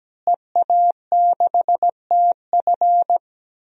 Morse code, heard as a keyed tone, about 17 wpm, 700 Hz. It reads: EA6TF
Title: DXCC-100 Call Signs - 5 Characters - Repeated 3x - 17wpm